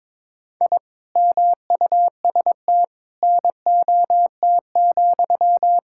Morse code, heard as a keyed tone, about 22 wpm, 700 Hz. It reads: I MUST NOT,
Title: Sentences from Top 1000 Words - Review Part 5 - 22wpm